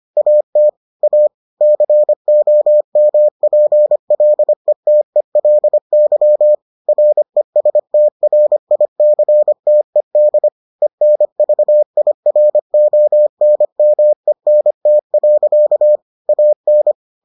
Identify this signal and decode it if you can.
Morse code, heard as a keyed tone, about 25 wpm, 600 Hz. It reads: AT A COMPLETELY RESTRICTED ENVIRONMENT. AN